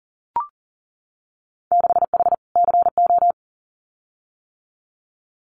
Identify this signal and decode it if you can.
Morse code, heard as a keyed tone, about 41 wpm, 700 Hz. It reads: 6H CO
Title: Winter Field Day Exchanges Encore 40wpm